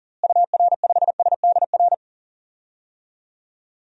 Morse code, heard as a keyed tone, about 40 wpm, 700 Hz. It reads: UR5SDR
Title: DXCC-100 Call Signs - 6 Characters - Repeated 3x - 40wpm